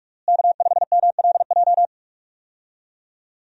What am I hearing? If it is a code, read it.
Morse code, heard as a keyed tone, about 46 wpm, 700 Hz. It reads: K5MLJ
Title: USA Call Signs - 5 Characters - Repeated 2x - 45wpm